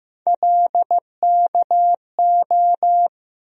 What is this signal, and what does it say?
Morse code, heard as a keyed tone, about 15 wpm, 700 Hz. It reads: LKO
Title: USA Call Sign Postfixes - 3 Letters 15wpm